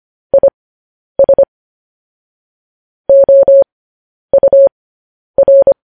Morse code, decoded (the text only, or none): IS OUR